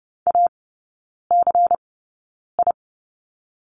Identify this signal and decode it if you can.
Morse code, heard as a keyed tone, about 30 wpm, spaced as for 9 wpm, 700 Hz. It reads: ACI
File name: Lesson 16e - Letter c - Three Characters 30wpm